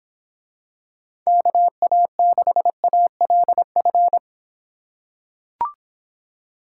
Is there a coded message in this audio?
Yes: Morse code from 1.3 to 4.2 s, heard as a keyed tone, about 24 wpm, 700 Hz. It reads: KA6ALF